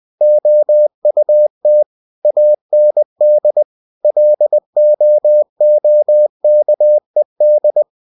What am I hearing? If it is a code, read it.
Morse code, heard as a keyed tone, about 20 wpm, 600 Hz. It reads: OUT AND LOOKED